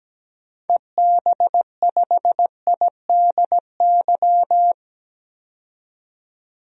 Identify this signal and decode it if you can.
Morse code, heard as a keyed tone, about 17 wpm, 700 Hz. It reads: EB5IDY
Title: DXCC-100 Call Signs - 6 Characters - 17wpm